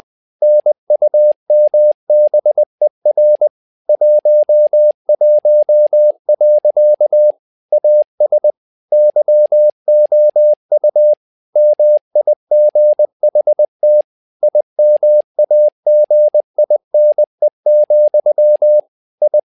Morse code, heard as a keyed tone, about 20 wpm, 600 Hz. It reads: NUMBER 11. AS YOU MIGHT IMAGINE, I
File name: tower trouble 20wpm